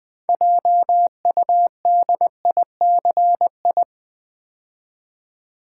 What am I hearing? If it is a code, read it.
Morse code, heard as a keyed tone, about 20 wpm, 700 Hz. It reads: JUDICI
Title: Build A Word - 3 Letters At A Time - 8-Letter Words - 20wpm